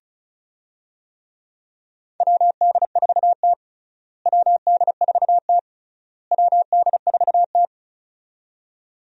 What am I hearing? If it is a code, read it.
Morse code, heard as a keyed tone, about 36 wpm, 700 Hz. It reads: WD4T WD4T WD4T
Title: USA Call Signs - 4 Characters - Repeated 3x - 35wpm